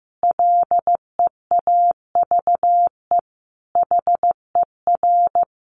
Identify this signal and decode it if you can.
Morse code, heard as a keyed tone, about 15 wpm, 700 Hz. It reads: LEAVE HER
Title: Sentences from Top 1300 Words 15wpm